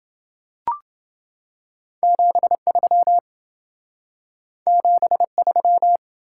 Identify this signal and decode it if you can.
Morse code, heard as a keyed tone, about 28 wpm, 700 Hz. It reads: 73 73